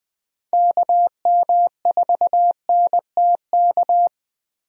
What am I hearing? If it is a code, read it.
Morse code, heard as a keyed tone, about 21 wpm, 700 Hz. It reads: KM4NTK
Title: USA Call Signs - 6 Characters - Repeated 2x - 20wpm